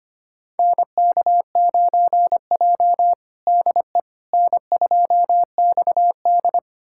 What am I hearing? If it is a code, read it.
Morse code, heard as a keyed tone, about 25 wpm, 700 Hz. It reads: NK9J DE N2XD
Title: USA Callsign DE Callsign - 4 Characters - 25wpm